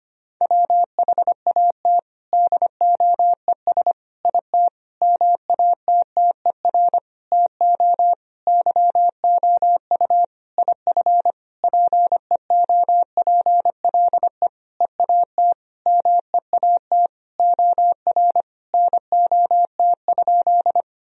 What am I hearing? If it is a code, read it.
Morse code, heard as a keyed tone, about 25 wpm, 700 Hz. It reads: WHAT DOES IT MATTER TO YOU IF PEOPLE EAT MEAT OR NOT?
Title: Sentences from Top 1000 Words - Review Part 3 - 25wpm